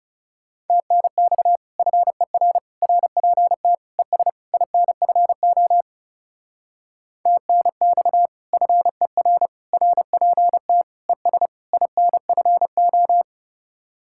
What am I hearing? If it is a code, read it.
Morse code, heard as a keyed tone, about 36 wpm, 700 Hz. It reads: TNX FER RPT ES INFO TNX FER RPT ES INFO
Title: Speed-Racing Common QSO Phrases - 30wpm